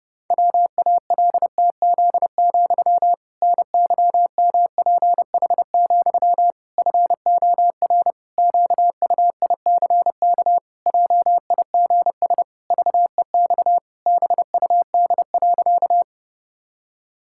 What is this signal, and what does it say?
Morse code, heard as a keyed tone, about 28 wpm, 700 Hz. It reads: WALTZ, NYMPH, FOR QUICK JIGS VEX BUD.
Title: Pangram Challenge v2 - 30wpm